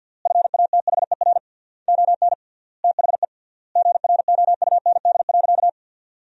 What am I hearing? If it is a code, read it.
Morse code, heard as a keyed tone, about 51 wpm, 700 Hz. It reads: FATHER ON THE GROUND.
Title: 3x Word-Spacing - Sentences from Top 500 Words - 50wpm